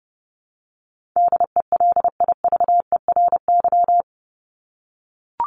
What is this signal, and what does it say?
Morse code, heard as a keyed tone, about 30 wpm, 700 Hz. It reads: DELIVERY
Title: Top 1251-1500 Words 30wpm